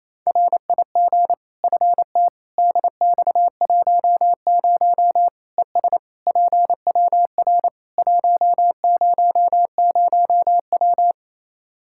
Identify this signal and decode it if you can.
Morse code, heard as a keyed tone, about 28 wpm, 700 Hz. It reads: RIG FT DX10 ES PWR 100W
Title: Practice QSO - Exchange 2 - Info - 28wpm